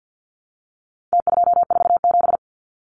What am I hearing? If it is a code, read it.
Morse code, heard as a keyed tone, about 50 wpm, 700 Hz. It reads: T247
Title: Numbers - Contest Style - 4-Digit 50wpm